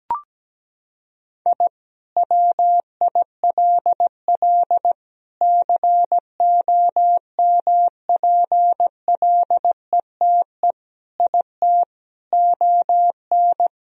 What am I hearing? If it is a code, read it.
Morse code, heard as a keyed tone, about 17 wpm, 700 Hz. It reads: I WILL COMPLETE IT ON
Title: Sentences from Top 1000 Words - Review Part 2 - 17wpm